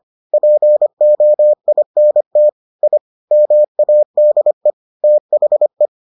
Morse code, heard as a keyed tone, about 25 wpm, 600 Hz. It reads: POINT I MADE THE